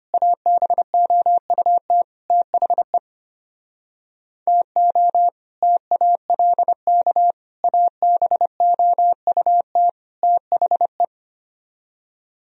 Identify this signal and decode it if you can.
Morse code, heard as a keyed tone, about 33 wpm, 700 Hz. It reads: ABOUT THE TO TALK ABOUT THE